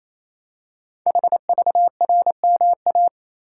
Morse code, heard as a keyed tone, about 27 wpm, 700 Hz. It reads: HVRMA
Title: Code Groups - 5 Letters - 28wpm